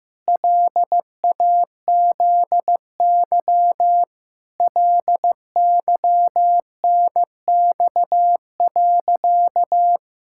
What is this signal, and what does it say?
Morse code, heard as a keyed tone, about 16 wpm, 700 Hz. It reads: LAZY LYNX.
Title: Pangram Challenge v2 - 15wpm